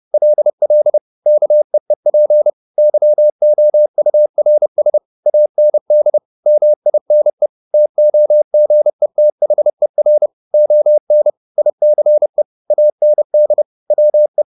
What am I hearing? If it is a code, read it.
Morse code, heard as a keyed tone, about 30 wpm, 600 Hz. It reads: LL KEEP YOURS AND MINE TOGETHER ON ICE AND WE
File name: the old man and the sea / the old man and the sea 1 30wpm